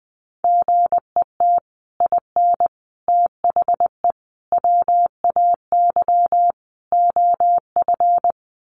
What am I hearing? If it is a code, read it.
Morse code, heard as a keyed tone, about 20 wpm, 700 Hz. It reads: GET IN THE WAY OF
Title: Sets of 5 Words Encore 20wpm